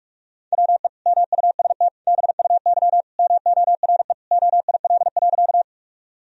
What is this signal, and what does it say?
Morse code, heard as a keyed tone, about 46 wpm, 700 Hz. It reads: WE MUST BUY MORE OIL.
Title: Sentences from Top 1000 Words - Review Part 4 - 45wpm